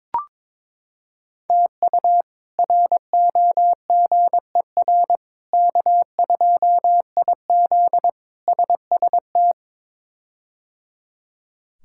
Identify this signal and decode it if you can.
Morse code, heard as a keyed tone, about 22 wpm, 700 Hz. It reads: TU ROGER K2IZ SST
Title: K1USN SST QSO Practice - 22wpm